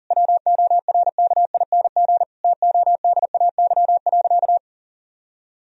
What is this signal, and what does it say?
Morse code, heard as a keyed tone, about 37 wpm, 700 Hz. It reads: WORKING TODAY.
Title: Sentences from Top 1000 Words - Review Part 4 - 40wpm